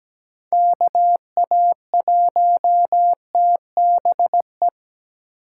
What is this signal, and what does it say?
Morse code, heard as a keyed tone, about 17 wpm, 700 Hz. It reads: KA1TBE
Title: USA Call Signs - 6 Characters - 3x Character Spacing - 17wpm